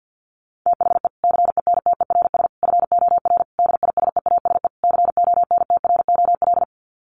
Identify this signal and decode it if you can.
Morse code, heard as a keyed tone, about 50 wpm, 700 Hz. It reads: THE CENTERS FOR DISEASE CONTROL
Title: Sets of 5 Words Encore2 50wpm